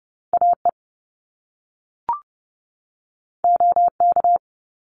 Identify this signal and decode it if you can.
Morse code, heard as a keyed tone, about 30 wpm, 700 Hz. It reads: AE OK
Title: Top Two Letter Combinations in Words 30wpm